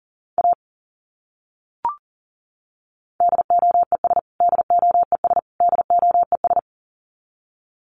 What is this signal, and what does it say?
Morse code, heard as a keyed tone, about 40 wpm, 700 Hz. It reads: A DOES DOES DOES
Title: Top 200 Words Repeated 40wpm